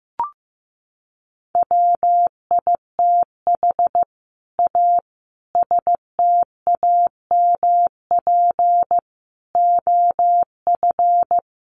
Morse code, heard as a keyed tone, about 15 wpm, 700 Hz. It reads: WITH A STAMP OF
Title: A Sound Of Thunder 15wpm